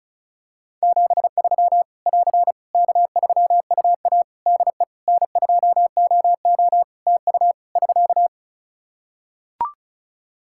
Morse code, heard as a keyed tone, about 36 wpm, 700 Hz. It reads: 73 <AR> K3UA DE N2OO TU <SK>
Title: K1USN SST QSO Practice 35wpm